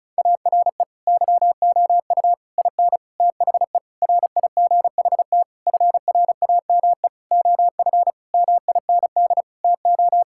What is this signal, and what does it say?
Morse code, heard as a keyed tone, about 36 wpm, 700 Hz. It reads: ARE YOU IN THE RIGHT FRAME OF MIND TO
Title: Sentences from Top 1000 Words - Review Part 6 - 35wpm